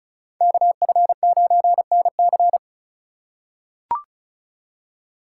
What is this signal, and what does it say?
Morse code, heard as a keyed tone, about 36 wpm, 700 Hz. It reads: KF9NC
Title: USA Call Signs - 5 Characters - 2x Character Spacing - 35wpm